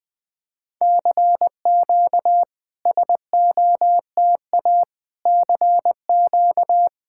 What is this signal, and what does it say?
Morse code, heard as a keyed tone, about 20 wpm, 690 Hz. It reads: CQ SOTA CQ
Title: SOTA QSO Practice 20wpm